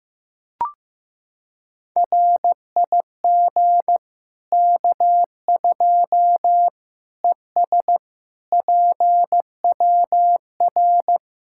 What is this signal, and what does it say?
Morse code, heard as a keyed tone, about 15 wpm, 700 Hz. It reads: RIG K2 ES PWR
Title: Practice QSO - Exchange 2 - Info - 15wpm